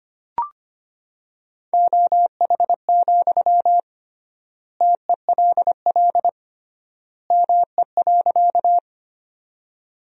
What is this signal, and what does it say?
Morse code, heard as a keyed tone, about 25 wpm, 700 Hz. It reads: OH, TELL ME.
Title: 3x Word-Spacing - Sentences from Top 500 Words - 25wpm